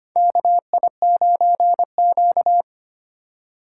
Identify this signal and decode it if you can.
Morse code, heard as a keyed tone, about 25 wpm, 700 Hz. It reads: KI9Q
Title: USA Call Signs - 3x Character Spacing - 25wpm